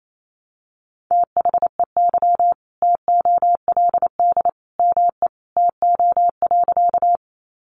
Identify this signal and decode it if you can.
Morse code, heard as a keyed tone, about 28 wpm, 700 Hz. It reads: THEY TOLD ME TO.
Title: Sentences from Top 400 Words 28wpm